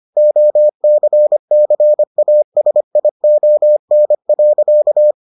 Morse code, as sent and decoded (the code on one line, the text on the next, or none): --- -.-. -.-. .- ... .. --- -. .-.-.-
OCCASION.